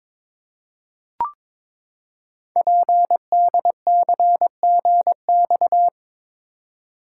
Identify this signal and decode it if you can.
Morse code, heard as a keyed tone, about 22 wpm, 700 Hz. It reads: PDCGX
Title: Code Groups - 5 Letters - 22wpm